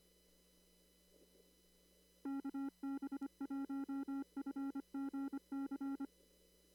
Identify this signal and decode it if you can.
Morse code, heard as a keyed tone, about 25 wpm, 280 Hz. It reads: KB1FGC
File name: CW / kt5j wpx cw-op-k5pi